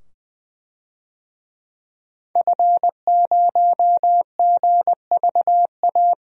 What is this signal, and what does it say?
Morse code, heard as a keyed tone, about 20 wpm, 700 Hz. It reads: F0GVA
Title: DXCC-100 Call Signs - 5 Characters - 20wpm